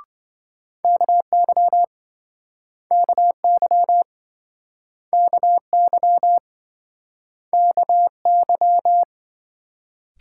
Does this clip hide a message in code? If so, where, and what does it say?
Yes: Morse code from 0.8 to 9.0 s, heard as a keyed tone, about 28 wpm, 700 Hz. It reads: KY KY KY KY